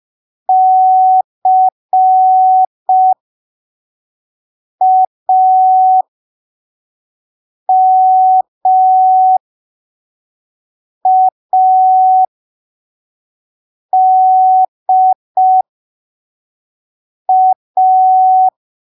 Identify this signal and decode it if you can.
Morse code, heard as a keyed tone, about 5 wpm, spaced as for 3 wpm, 750 Hz. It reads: CAMADA
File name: arquivos / texto / texto 11 05ppm